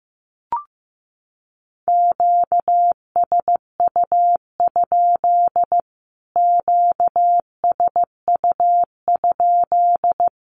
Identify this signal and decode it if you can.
Morse code, heard as a keyed tone, about 15 wpm, 700 Hz. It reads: QSU? QSU?
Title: Q-Codes Repeated 3x 15wpm